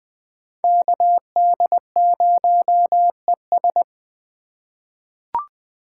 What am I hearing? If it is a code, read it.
Morse code, heard as a keyed tone, about 20 wpm, 700 Hz. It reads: KD0ES
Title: USA Call Signs - 5 Characters - 2x Character Spacing - 20wpm